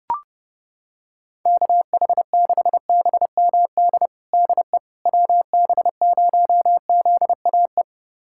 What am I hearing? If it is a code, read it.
Morse code, heard as a keyed tone, about 32 wpm, 690 Hz. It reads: KH6BMD DE WB0ZAE
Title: USA Callsign DE Callsign - 6 Characters - 30wpm